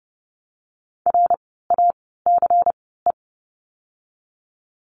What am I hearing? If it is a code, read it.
Morse code, heard as a keyed tone, about 30 wpm, 700 Hz. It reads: R A C E
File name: Lesson 16f - Letter c - Words 30wpm